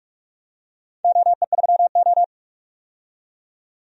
Morse code, heard as a keyed tone, about 44 wpm, 700 Hz. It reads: OE3O
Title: DXCC-100 Call Signs - 4 Characters - 45wpm